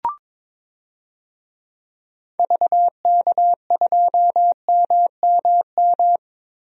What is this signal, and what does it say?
Morse code, heard as a keyed tone, about 22 wpm, 700 Hz. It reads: VK2MMM
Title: DXCC-100 Call Signs - 6 Characters - 22wpm